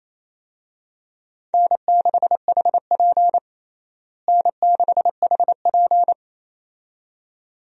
Morse code, heard as a keyed tone, about 28 wpm, 700 Hz. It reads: N6HP N6HP
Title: USA Call Signs - 4 Characters - Repeated 2x - 28wpm